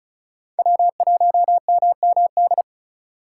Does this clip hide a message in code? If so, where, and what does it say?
Yes: Morse code from 0.6 to 2.6 s, heard as a keyed tone, about 44 wpm, 700 Hz. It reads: W1MMD